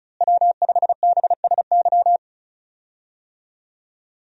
Morse code, heard as a keyed tone, about 36 wpm, 700 Hz. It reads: W5BSY
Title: USA Call Signs - 5 Characters - Repeated 2x - 35wpm